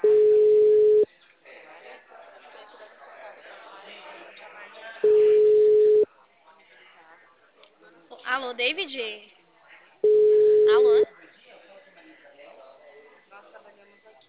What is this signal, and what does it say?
Ringback tone (Europe, 425 Hz), with speech.
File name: out- -2015- - - .31101